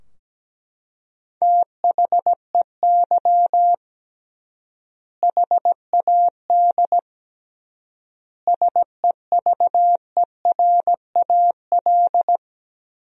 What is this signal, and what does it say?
Morse code, heard as a keyed tone, about 17 wpm, 700 Hz. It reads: THEY HAD SEVERAL
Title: 3x Word-Spacing - Sentences from Top 400 Words - 17wpm